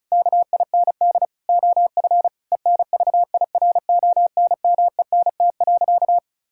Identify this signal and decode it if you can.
Morse code, heard as a keyed tone, about 36 wpm, 700 Hz. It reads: KIND OF ENVIRONMENT.
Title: Sentences from Top 1000 Words - Review Part 5 - 35wpm